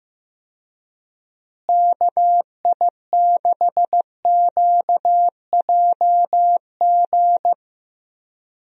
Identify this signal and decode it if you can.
Morse code, heard as a keyed tone, about 15 wpm, 700 Hz. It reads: KI6QJG
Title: USA Call Signs - 6 Characters - Repeated 2x - 15wpm